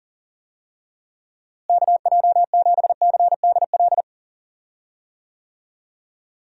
Morse code, heard as a keyed tone, about 40 wpm, 700 Hz. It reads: KJ7CDL
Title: USA Call Signs - 6 Characters - Repeated 3x - 40wpm